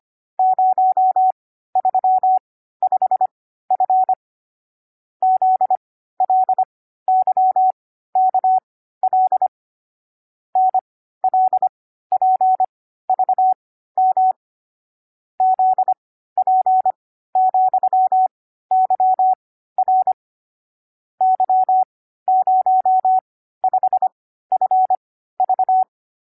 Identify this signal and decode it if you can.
Morse code, heard as a keyed tone, about 25 wpm, spaced as for 14 wpm, 750 Hz. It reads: 035F ZLYKL NLPVM ZP,YR Y05FV
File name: CW-Koch / koch.29.3.groups5.15wpm